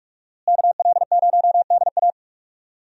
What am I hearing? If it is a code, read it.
Morse code, heard as a keyed tone, about 46 wpm, 700 Hz. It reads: KL0DA
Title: USA Call Signs - 5 Characters - Repeated 2x - 45wpm